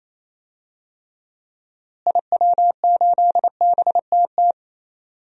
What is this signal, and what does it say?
Morse code, heard as a keyed tone, about 28 wpm, 700 Hz. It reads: IW8BTT
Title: DXCC-100 Call Signs - 6 Characters - Repeated 3x - 28wpm